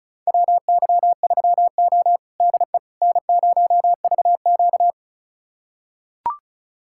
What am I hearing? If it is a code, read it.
Morse code, heard as a keyed tone, about 36 wpm, 700 Hz. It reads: WY3O DE N0VQ
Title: USA Callsign DE Callsign - 4 Characters - 35wpm